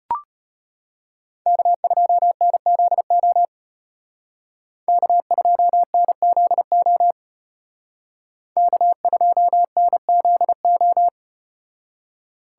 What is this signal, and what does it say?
Morse code, heard as a keyed tone, about 32 wpm, 700 Hz. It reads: K2NZO K2NZO K2NZO